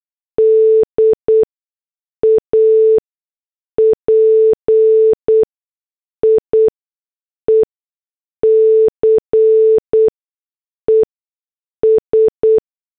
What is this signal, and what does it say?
Morse code, heard as a keyed tone, about 8 wpm, spaced as for 6 wpm, 440 Hz. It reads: D A PIECES
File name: morse / headlines